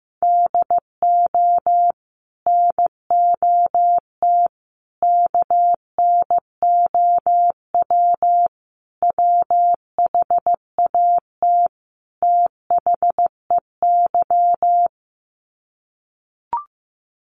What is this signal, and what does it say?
Morse code, heard as a keyed tone, about 14 wpm, 700 Hz. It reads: DO NOT KNOW WHAT THEY
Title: Sets of 5 Words 15wpm